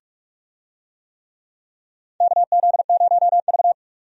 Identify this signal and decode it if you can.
Morse code, heard as a keyed tone, about 46 wpm, 700 Hz. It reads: KZ0V
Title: USA Call Signs - 4 Characters - 45wpm